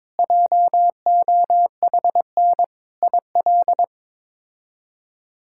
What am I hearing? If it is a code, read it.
Morse code, heard as a keyed tone, about 22 wpm, 700 Hz. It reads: JOHN IL
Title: K1USN SST QSO Practice - 22wpm